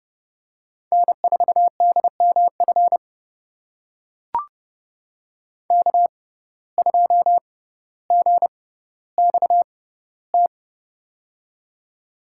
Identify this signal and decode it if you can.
Morse code, heard as a keyed tone, about 30 wpm, spaced as for 15 wpm, 700 Hz. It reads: N4DMF K2GXT
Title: USA Call Signs - 6x Character Spacing - 30wpm